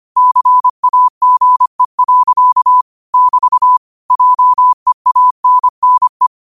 Morse code, heard as a keyed tone, about 25 wpm, 1000 Hz. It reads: CAGE. <BT> JEANNE
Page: Quotes for Sat, 16 Aug 2025 in Morse Code at 25 words per minute.